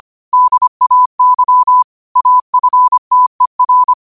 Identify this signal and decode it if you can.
Morse code, heard as a keyed tone, about 25 wpm, 1000 Hz. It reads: DAY AFTER
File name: 25wpm